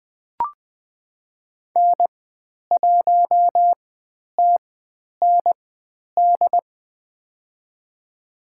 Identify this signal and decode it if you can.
Morse code, heard as a keyed tone, about 20 wpm, spaced as for 10 wpm, 700 Hz. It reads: N1TND
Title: 102 Lesson 17g - Letter u - Call Signs 20wpm